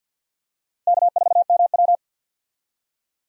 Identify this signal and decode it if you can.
Morse code, heard as a keyed tone, about 53 wpm, 700 Hz. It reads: K4MW
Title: USA Call Signs - 4 Characters - 50wpm